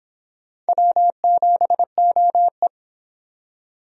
Morse code, heard as a keyed tone, about 26 wpm, 700 Hz. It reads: W7OE